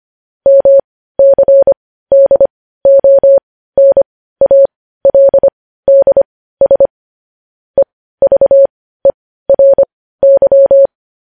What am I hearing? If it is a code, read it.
Morse code, heard as a keyed tone, about 25 wpm, spaced as for 15 wpm, 550 Hz. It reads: MCDONALDS EVERY
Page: news headlines in morse code at 15 wpm